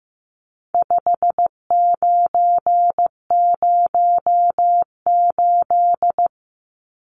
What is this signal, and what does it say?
Morse code, heard as a keyed tone, about 15 wpm, 700 Hz. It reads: 5908
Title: Numbers - 4-Digit 15wpm